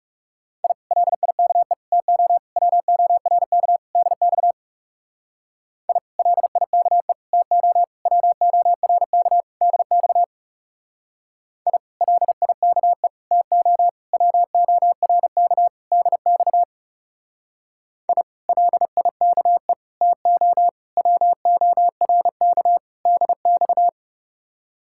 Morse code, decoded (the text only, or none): I LIKE TO WORK DX I LIKE TO WORK DX I LIKE TO WORK DX I LIKE TO WORK DX